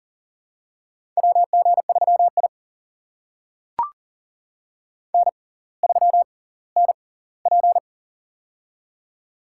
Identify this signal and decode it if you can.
Morse code, heard as a keyed tone, about 40 wpm, spaced as for 20 wpm, 700 Hz. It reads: WG3I N3NP
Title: USA Call Signs - 4 Characters - 6x Character Spacing - 40wpm